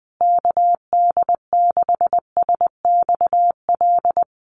Morse code, heard as a keyed tone, about 20 wpm, 700 Hz. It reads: KD6SXL
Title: USA Call Signs Repeated 2x 20wpm